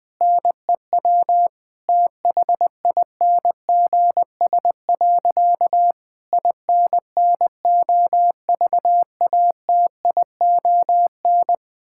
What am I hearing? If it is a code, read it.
Morse code, heard as a keyed tone, about 21 wpm, 700 Hz. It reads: NEW THINGS. INNOVATION
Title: Difficult Sentences with Hints - 20wpm